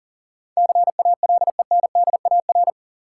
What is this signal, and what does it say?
Morse code, heard as a keyed tone, about 40 wpm, 700 Hz. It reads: CALENDAR
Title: Sentences from Top 1600 Words - New Word Drill - 40wpm